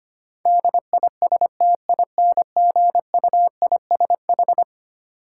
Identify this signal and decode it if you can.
Morse code, heard as a keyed tone, about 25 wpm, 700 Hz. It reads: DISTINGUISH
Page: morse-code-ninja-practice-25wpm